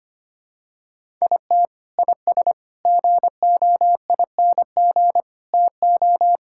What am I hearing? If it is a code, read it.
Morse code, heard as a keyed tone, about 25 wpm, 700 Hz. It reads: IT IS GOING TO
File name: practice / 25wpm / Sentences with 3-Word Phrases - 25wpm